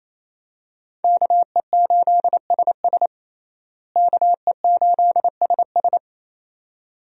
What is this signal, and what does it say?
Morse code, heard as a keyed tone, about 28 wpm, 700 Hz. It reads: KE8SS KE8SS
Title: USA Call Signs - 5 Characters - Repeated 2x - 28wpm